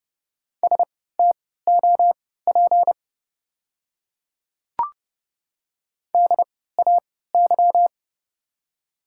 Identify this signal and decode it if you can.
Morse code, heard as a keyed tone, about 30 wpm, spaced as for 17 wpm, 700 Hz. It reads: STOP DAY
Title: 3x Character Spacing - Top 300 Words - 30wpm